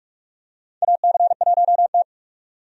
Morse code, heard as a keyed tone, about 41 wpm, 700 Hz. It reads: AC1T
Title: USA Call Signs - 4 Characters - 45wpm